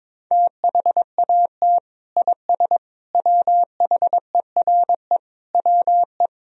Morse code, heard as a keyed tone, about 22 wpm, 700 Hz. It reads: THAT IS WHERE WE
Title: Sentences from Top 1000 Words - Review Part 1 - 22wpm